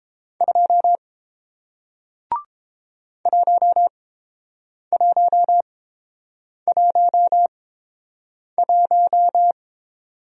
Morse code, heard as a keyed tone, about 30 wpm, spaced as for 7 wpm, 700 Hz. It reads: 2 1111